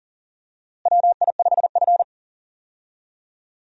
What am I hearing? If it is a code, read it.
Morse code, heard as a keyed tone, about 40 wpm, 700 Hz. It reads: WI5F
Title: USA Call Signs - 4 Characters - Repeated 3x - 40wpm